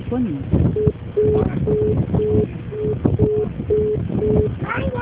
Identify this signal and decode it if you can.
Congestion tone (425 Hz), with speech.